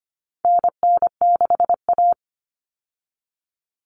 Morse code, heard as a keyed tone, about 25 wpm, 700 Hz. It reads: NN6A
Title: Common Cut Numbers - 4-Digit 25wpm